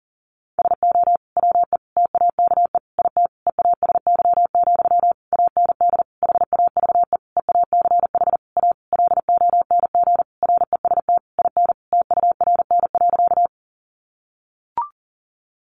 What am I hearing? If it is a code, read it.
Morse code, heard as a keyed tone, about 37 wpm, 700 Hz. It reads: SO WE TAKE IT EASY, AND HAVE EACH A LONG REST IN TURN.
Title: Sentences from Top 700 Words 40wpm